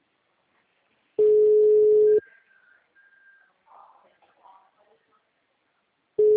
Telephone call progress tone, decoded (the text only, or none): ringback tone